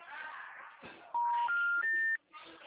Special information tone, with noise.